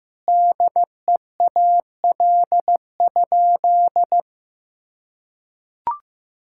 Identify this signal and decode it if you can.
Morse code, heard as a keyed tone, about 15 wpm, 700 Hz. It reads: DEAL?
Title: Sentences from Top 1000 Words - Review Part 6 - 15wpm